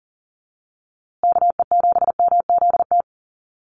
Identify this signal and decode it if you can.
Morse code, heard as a keyed tone, about 40 wpm, 700 Hz. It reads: KE7MZT
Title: USA Call Signs Repeated 2x 40wpm